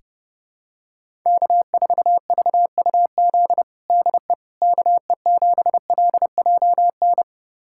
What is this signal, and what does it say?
Morse code, heard as a keyed tone, about 28 wpm, 700 Hz. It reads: K4VUZ DE KE7LJN
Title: USA Callsign DE Callsign 30wpm